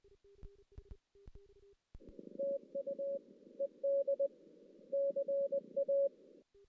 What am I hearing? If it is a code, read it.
Morse code, heard as a keyed tone, about 20 wpm, 560 Hz. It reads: TU ED CA
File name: 2016 JAN CW NAQP / 2016 Jan naqp k5ot - 0-5